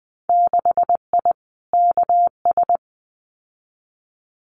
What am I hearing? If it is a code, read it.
Morse code, heard as a keyed tone, about 20 wpm, 700 Hz. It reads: 6I KS
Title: Winter Field Day Exchanges 20wpm